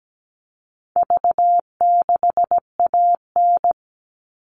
Morse code, heard as a keyed tone, about 17 wpm, 700 Hz. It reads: V6AN
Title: All Cut Numbers - 4-Digit 17wpm